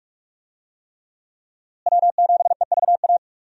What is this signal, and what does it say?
Morse code, heard as a keyed tone, about 46 wpm, 700 Hz. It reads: W7EVA
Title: USA Call Signs - 5 Characters - Repeated 3x - 45wpm